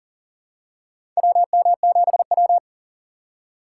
Morse code, heard as a keyed tone, about 40 wpm, 700 Hz. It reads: WM7W
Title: USA Call Signs - 4 Characters - 4x Character Spacing - 40wpm